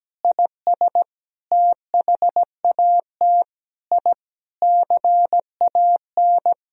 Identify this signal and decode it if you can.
Morse code, heard as a keyed tone, about 17 wpm, 700 Hz. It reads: IS THAT I CAN
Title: Sentences from Top 1000 Words - Review Part 3 - 17wpm